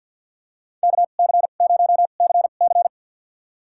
Morse code, heard as a keyed tone, about 48 wpm, spaced as for 35 wpm, 700 Hz. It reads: KC0CC
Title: USA Call Signs - 5 Characters - 2x Character Spacing - 50wpm